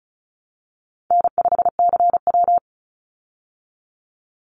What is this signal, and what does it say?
Morse code, heard as a keyed tone, about 34 wpm, 700 Hz. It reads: N5CW
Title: Easy Call Signs Repeated 3x - 35wpm